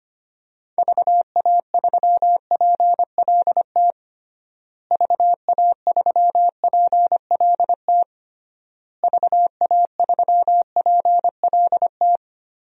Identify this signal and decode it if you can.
Morse code, heard as a keyed tone, about 25 wpm, 700 Hz. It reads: VA3PLT VA3PLT VA3PLT
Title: DXCC-100 Call Signs - 6 Characters - Repeated 3x - 25wpm